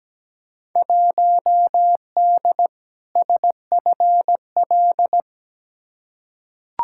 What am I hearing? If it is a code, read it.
Morse code, heard as a keyed tone, about 17 wpm, 700 Hz. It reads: 1D SFL
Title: Field Day Exchanges - 17wpm